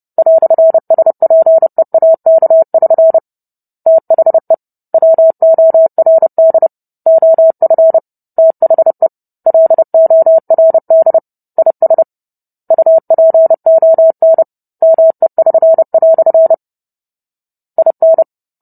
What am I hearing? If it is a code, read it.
Morse code, heard as a keyed tone, about 30 wpm, 650 Hz. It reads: "SPEAK<SN> THE WORD OF THE LORD IS UPON ME<SN>" IN